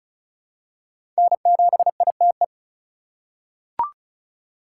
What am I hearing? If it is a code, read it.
Morse code, heard as a keyed tone, about 36 wpm, 700 Hz. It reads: N7ITE
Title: USA Call Signs - 5 Characters - 3x Character Spacing - 35wpm